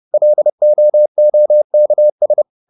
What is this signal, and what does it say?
Morse code, heard as a keyed tone, about 30 wpm, 600 Hz. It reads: LOOKS